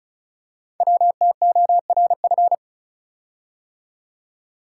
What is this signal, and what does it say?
Morse code, heard as a keyed tone, about 36 wpm, 700 Hz. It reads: WTORF
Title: Code Groups - 5 Letter:Numbers - 35wpm